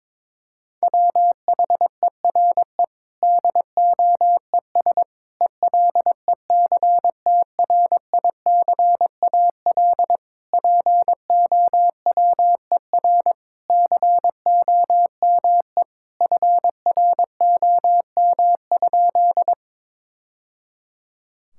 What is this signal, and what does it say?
Morse code, heard as a keyed tone, about 22 wpm, 700 Hz. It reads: WHERE DOES ELECTRICAL POWER COME FROM?
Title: Sentences from Top 1000 Words - Review Part 6 - 22wpm